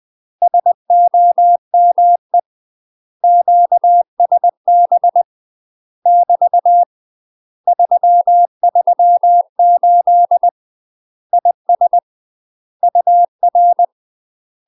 Morse code, heard as a keyed tone, about 20 wpm, 700 Hz. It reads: SOME QSB <BT> 338 IS UR
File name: UK MASTER QSOs TRAINING / 20WPM / UK MASTER QSOs 0073